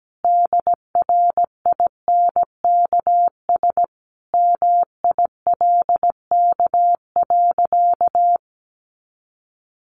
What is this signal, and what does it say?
Morse code, heard as a keyed tone, about 17 wpm, 700 Hz. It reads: DRINKS MILK.
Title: Easy Phrases 17wpm